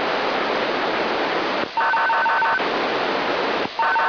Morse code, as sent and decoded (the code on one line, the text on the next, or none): none